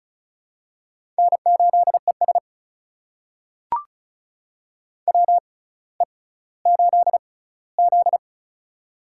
Morse code, heard as a keyed tone, about 36 wpm, spaced as for 18 wpm, 700 Hz. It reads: N8ES WE8Z
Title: USA Call Signs - 4 Characters - 6x Character Spacing - 35wpm